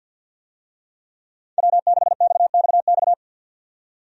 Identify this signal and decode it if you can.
Morse code, heard as a keyed tone, about 51 wpm, 700 Hz. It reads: W6XXX
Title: USA Call Signs - 5 Characters - 50wpm